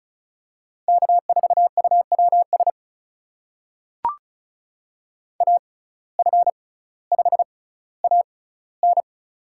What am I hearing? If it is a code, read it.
Morse code, heard as a keyed tone, about 36 wpm, spaced as for 18 wpm, 700 Hz. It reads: K4UWS AF5AN
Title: USA Call Signs - 5 Characters - 6x Character Spacing - 35wpm